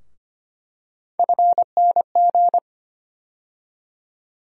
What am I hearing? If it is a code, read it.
Morse code, heard as a keyed tone, about 25 wpm, 700 Hz. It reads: FNG
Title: USA Call Sign Postfixes - 3 Letters 25wpm